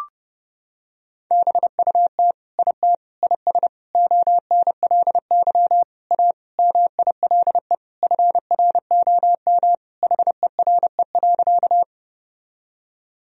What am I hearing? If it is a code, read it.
Morse code, heard as a keyed tone, about 30 wpm, 700 Hz. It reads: BUT IT IS ONLY A MILE FROM HERE.
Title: Sentences from Top 1000 Words - Review Part 1 - 30wpm